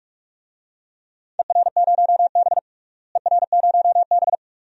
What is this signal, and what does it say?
Morse code, heard as a keyed tone, about 47 wpm, 700 Hz. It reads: ER0B ER0B
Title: DXCC-100 Call Signs - 4 Characters - Repeated 2x - 45wpm